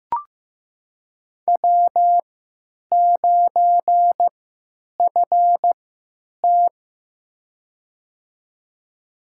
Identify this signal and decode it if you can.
Morse code, heard as a keyed tone, about 15 wpm, spaced as for 9 wpm, 700 Hz. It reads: W9FT
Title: USA Call Signs - 3x Character Spacing - 15wpm